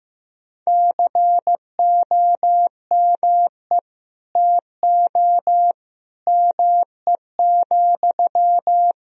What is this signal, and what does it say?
Morse code, heard as a keyed tone, about 15 wpm, 700 Hz. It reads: COME TO ME,
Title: Sentences from Top 1000 Words - Review Part 2 - 15wpm